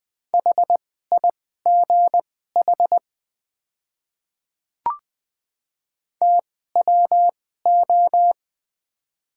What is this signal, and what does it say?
Morse code, heard as a keyed tone, about 20 wpm, spaced as for 15 wpm, 690 Hz. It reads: HIGH TWO
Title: 2x Character Spacing - Top 200 Words - 20wpm